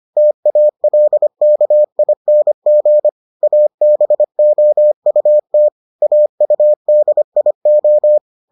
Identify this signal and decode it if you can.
Morse code, heard as a keyed tone, about 25 wpm, 600 Hz. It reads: TALKING ABOUT AUDIO